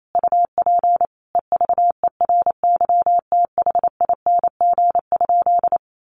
Morse code, heard as a keyed tone, about 28 wpm, 700 Hz. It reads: UP EVERYTHING?
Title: Sentences from Top 700 Words 28wpm